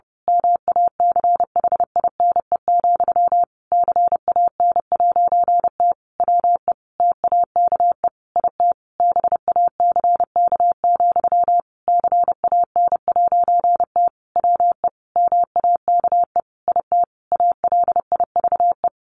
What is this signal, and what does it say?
Morse code, heard as a keyed tone, about 30 wpm, 700 Hz. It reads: MACHINE, CAN'T WE TAKE IT BACK, CAN'T WE MAKE IT ALIVE
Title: A Sound Of Thunder 30wpm